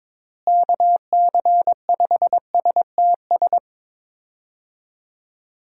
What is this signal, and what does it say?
Morse code, heard as a keyed tone, about 22 wpm, 700 Hz. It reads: KC5STS
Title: USA Call Signs - 6 Characters - Repeated 3x - 22wpm